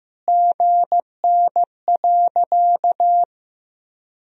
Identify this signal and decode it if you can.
Morse code, heard as a keyed tone, about 15 wpm, 700 Hz. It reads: GN.
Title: Sentences from Top 300 Words and Abbreviations 15wpm